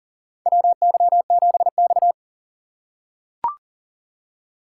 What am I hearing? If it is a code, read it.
Morse code, heard as a keyed tone, about 39 wpm, 700 Hz. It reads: WY7X
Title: USA Call Signs - 4 Characters - 2x Character Spacing - 40wpm